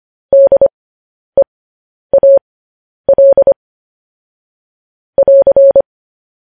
Morse code, heard as a keyed tone, about 25 wpm, spaced as for 10 wpm, 550 Hz. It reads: DEAL <AR>